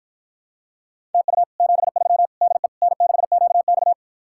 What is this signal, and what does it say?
Morse code, heard as a keyed tone, about 54 wpm, 700 Hz. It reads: TU 73 DE N6QX